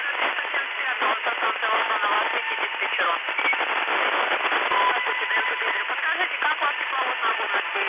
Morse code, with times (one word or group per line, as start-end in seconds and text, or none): none